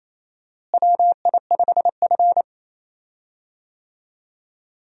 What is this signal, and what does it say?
Morse code, heard as a keyed tone, about 28 wpm, 700 Hz. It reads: WI5F
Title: USA Call Signs - 4 Characters - Repeated 3x - 28wpm